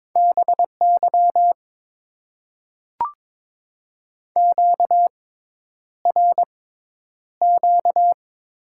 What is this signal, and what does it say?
Morse code, heard as a keyed tone, about 22 wpm, spaced as for 10 wpm, 700 Hz. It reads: BY QRQ
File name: Top 90 QSO Elements - 6x Character Spacing - 22wpm